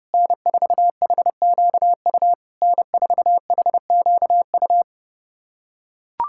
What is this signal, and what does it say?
Morse code, heard as a keyed tone, about 32 wpm, 690 Hz. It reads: N4HQU N4HQU
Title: SOTA QSO Practice 30wpm